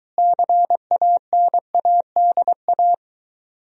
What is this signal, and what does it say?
Morse code, heard as a keyed tone, about 23 wpm, 700 Hz. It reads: CANADA